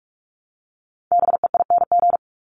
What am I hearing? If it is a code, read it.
Morse code, heard as a keyed tone, about 44 wpm, 700 Hz. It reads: BEING
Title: Top 501-750 Words 45wpm